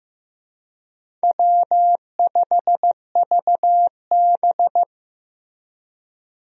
Morse code, heard as a keyed tone, about 15 wpm, 700 Hz. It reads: W5VB
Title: USA Call Signs - 4 Characters - Repeated 2x - 15wpm